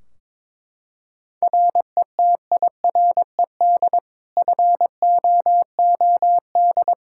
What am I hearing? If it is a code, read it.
Morse code, heard as a keyed tone, about 22 wpm, 700 Hz. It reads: RETIRED FOOD
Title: Practice QSO - Exchange 3 - More Info - 22wpm